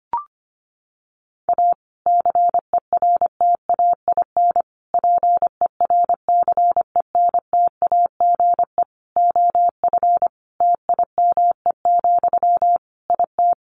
Morse code, heard as a keyed tone, about 25 wpm, 700 Hz. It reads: A CERTAIN PERCENTAGE OF TIME, IT
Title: Sentences from Top 1300 Words 25wpm